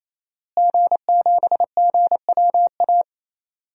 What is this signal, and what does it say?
Morse code, heard as a keyed tone, about 27 wpm, 700 Hz. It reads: G7GWA
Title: DXCC-100 Call Signs - 5 Characters - Repeated 3x - 28wpm